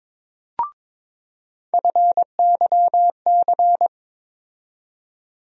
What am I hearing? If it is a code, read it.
Morse code, heard as a keyed tone, about 22 wpm, 700 Hz. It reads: FYC
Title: USA Call Sign Postfixes - 3 Letters 22wpm